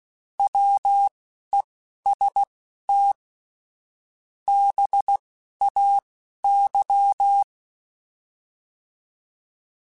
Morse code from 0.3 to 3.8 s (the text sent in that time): WEST